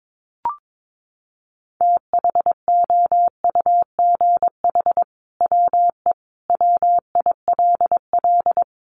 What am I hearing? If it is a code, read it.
Morse code, heard as a keyed tone, about 22 wpm, 700 Hz. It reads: THOUGH WE WILL
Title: Sentences from Top 1000 Words 22wpm